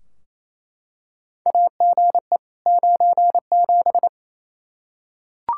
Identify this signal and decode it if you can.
Morse code, heard as a keyed tone, about 27 wpm, 700 Hz. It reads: AGE 97
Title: Practice QSO - Exchange 3 - More Info - 28wpm